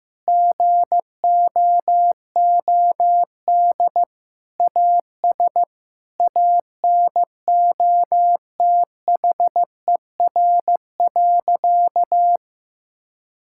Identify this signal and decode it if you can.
Morse code, heard as a keyed tone, about 15 wpm, 700 Hz. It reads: GOOD AS ANOTHER.
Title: Sentences from Top 1600 Words 15wpm